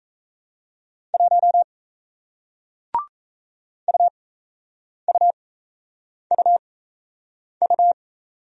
Morse code, heard as a keyed tone, about 37 wpm, spaced as for 8 wpm, 700 Hz. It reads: 1 UUUU